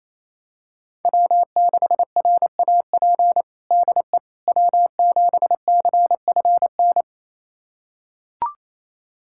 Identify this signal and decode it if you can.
Morse code, heard as a keyed tone, about 28 wpm, 700 Hz. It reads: W6RAP DE W7CFN
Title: USA Callsign DE Callsign - 5 Characters - 28wpm